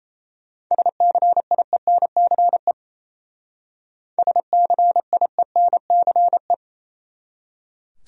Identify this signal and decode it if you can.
Morse code, heard as a keyed tone, about 36 wpm, 700 Hz. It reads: SCIENCE SCIENCE